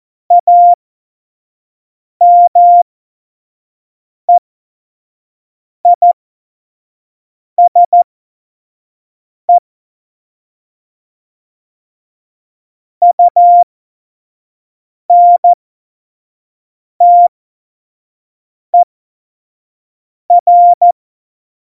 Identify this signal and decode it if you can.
Morse code, heard as a keyed tone, about 14 wpm, spaced as for 5 wpm, 700 Hz. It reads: AMEISE UNTER
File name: Morsekurs / Witze / Witz 15. Mai 2018, 25 F70